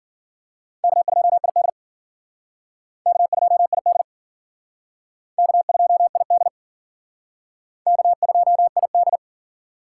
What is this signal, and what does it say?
Morse code, heard as a keyed tone, about 50 wpm, 700 Hz. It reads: K2ID K2ID K2ID K2ID